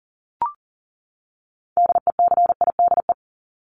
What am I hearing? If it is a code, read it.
Morse code, heard as a keyed tone, about 40 wpm, 700 Hz. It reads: DECIDE
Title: Top 500 Words 40wpm